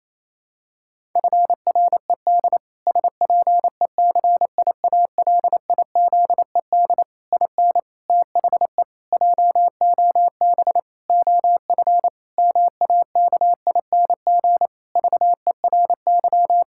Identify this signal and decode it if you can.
Morse code, heard as a keyed tone, about 28 wpm, 700 Hz. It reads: FRED SPECIALIZED IN THE JOB OF MAKING VERY
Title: Pangram Challenge v2 - 28wpm